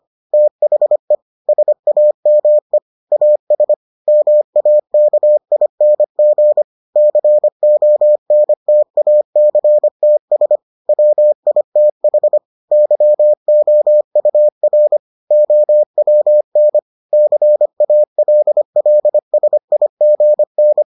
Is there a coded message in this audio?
Yes: Morse code from 0.3 to 20.8 s, heard as a keyed tone, about 25 wpm, 600 Hz. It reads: THE SAME AS MAKING CONTACTS WITH YOUR OWN CALLSIGN